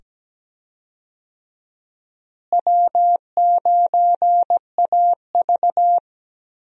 Morse code, heard as a keyed tone, about 17 wpm, 700 Hz. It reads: W9AV
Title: USA Call Signs - 4 Characters - Repeated 2x - 17wpm